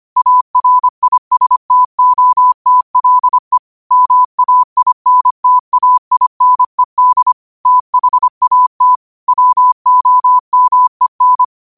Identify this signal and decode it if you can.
Morse code, heard as a keyed tone, about 25 wpm, 1000 Hz. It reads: ARISTOTLE MAINTAINED THAT WOMEN
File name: qotdfiles / 25wpm